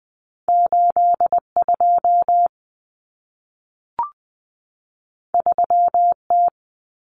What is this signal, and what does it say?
Morse code, heard as a keyed tone, about 20 wpm, 700 Hz. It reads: 82 3T
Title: Common Cut Numbers - 2-Digit 20wpm